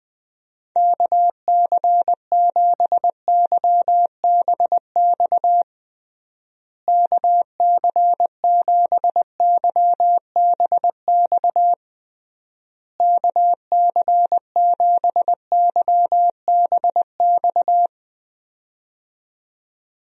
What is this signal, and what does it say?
Morse code, heard as a keyed tone, about 20 wpm, 700 Hz. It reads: KC7YBX KC7YBX KC7YBX
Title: USA Call Signs - 6 Characters - Repeated 3x - 20wpm